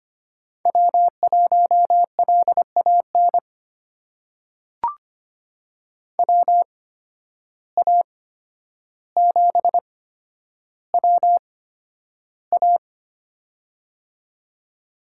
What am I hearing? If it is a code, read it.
Morse code, heard as a keyed tone, about 25 wpm, spaced as for 11 wpm, 700 Hz. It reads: W1LAN WA7WA
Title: USA Call Signs - 5 Characters - 8x Character Spacing - 25wpm